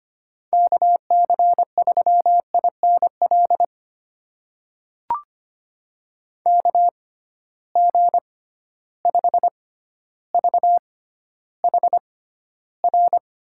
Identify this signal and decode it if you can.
Morse code, heard as a keyed tone, about 25 wpm, spaced as for 13 wpm, 700 Hz. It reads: KC3INL KG5VHR
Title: USA Call Signs - 6 Characters - 6x Character Spacing - 25wpm